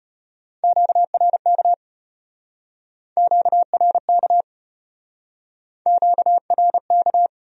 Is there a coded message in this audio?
Yes: Morse code from 0.6 to 7.3 s, heard as a keyed tone, about 36 wpm, 700 Hz. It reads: QRK QRK QRK